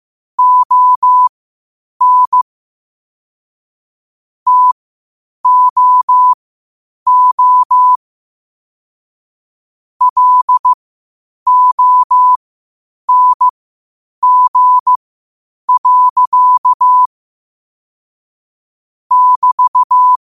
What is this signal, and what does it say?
Morse code, heard as a keyed tone, about 15 wpm, spaced as for 8 wpm, 1000 Hz. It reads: ON TOO LONG. <BT>
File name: qotdfiles / 8wpm